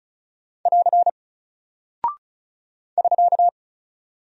Morse code, heard as a keyed tone, about 36 wpm, 700 Hz. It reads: <AR> <SK>
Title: Single Character (Letter:Number:Punctuation:Prosign) Alt 35wpm